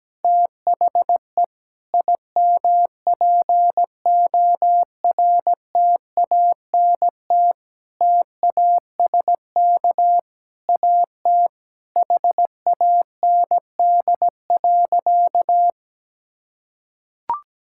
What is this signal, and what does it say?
Morse code, heard as a keyed tone, about 17 wpm, 700 Hz. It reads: THE IMPORTANT TASK AT HAND.
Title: Sentences from Top 1600 Words 17wpm